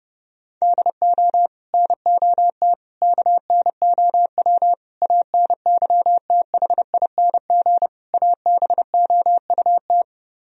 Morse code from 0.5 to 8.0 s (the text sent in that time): DO NOT KNOW ANYTHING